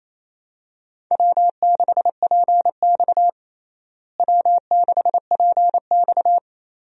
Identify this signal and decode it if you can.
Morse code, heard as a keyed tone, about 28 wpm, 700 Hz. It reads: W6PX W6PX
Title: USA Call Signs - 4 Characters - Repeated 3x - 28wpm